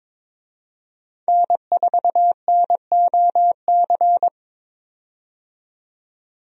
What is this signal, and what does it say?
Morse code, heard as a keyed tone, about 22 wpm, 700 Hz. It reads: N4NOC
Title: USA Call Signs - 5 Characters - 22wpm